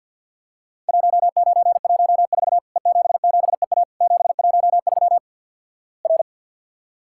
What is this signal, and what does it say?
Morse code, heard as a keyed tone, about 50 wpm, 700 Hz. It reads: 1914 E77EA 713 R
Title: Worked All Europe DX Contest - QTC Exchange with Repeats - 50wpm